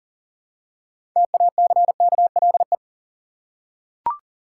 Morse code, heard as a keyed tone, about 40 wpm, 700 Hz. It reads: TACKLE
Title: Six Letter Words - 40wpm